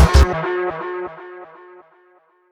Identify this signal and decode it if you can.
There is no code.